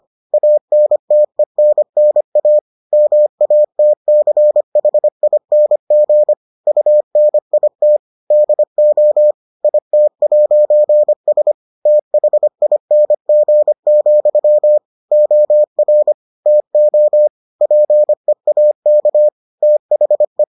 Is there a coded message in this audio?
Yes: Morse code from 0.3 to 20.4 s, heard as a keyed tone, about 25 wpm, 600 Hz. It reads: ANTENNA MATCHING UNIT DO IT'S THING, OR TO PEAK THE